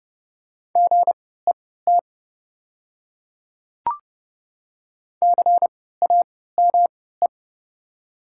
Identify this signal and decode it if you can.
Morse code, heard as a keyed tone, about 30 wpm, spaced as for 17 wpm, 700 Hz. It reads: GET CAME
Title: 3x Character Spacing - Top 200 Words - 30wpm